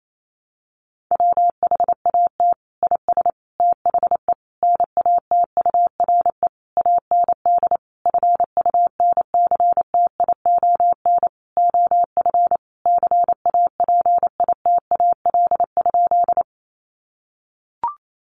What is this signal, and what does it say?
Morse code, heard as a keyed tone, about 28 wpm, 700 Hz. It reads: WHAT IS THE NATURE AND FUNCTION OF CAPITAL?
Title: Sentences from Top 1100 Words 28wpm